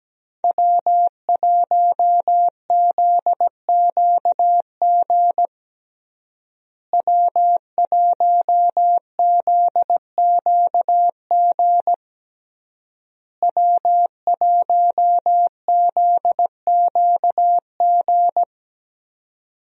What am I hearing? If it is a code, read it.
Morse code, heard as a keyed tone, about 17 wpm, 700 Hz. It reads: W1ZQG W1ZQG W1ZQG
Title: USA Call Signs - 5 Characters - Repeated 3x - 17wpm